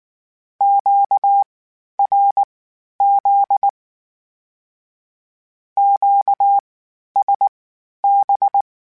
Morse code, heard as a keyed tone, about 19 wpm, spaced as for 11 wpm, 800 Hz. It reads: QRZ QSB
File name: Q Calls Twice sp @18